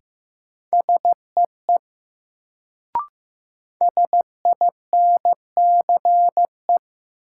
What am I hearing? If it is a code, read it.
Morse code, heard as a keyed tone, about 15 wpm, 700 Hz. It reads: SEE SINCE
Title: Sets of 2 Words - New Word Drill - 15wpm